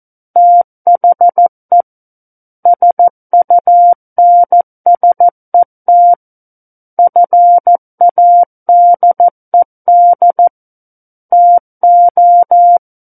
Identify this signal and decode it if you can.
Morse code, heard as a keyed tone, about 14 wpm, 700 Hz. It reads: THE SUNSET FADED TO
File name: Book One / 05-Chapter 5 - 14 WPM